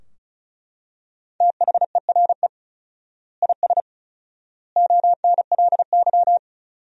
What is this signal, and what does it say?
Morse code, heard as a keyed tone, about 34 wpm, 700 Hz. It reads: THERE IS ONLY
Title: 4x Word-Spacing - Sentences from Top 300 Words - 35wpm